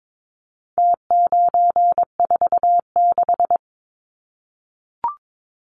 Morse code, heard as a keyed tone, about 22 wpm, 700 Hz. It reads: T946
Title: Numbers - Contest Style - 4-Digit 22wpm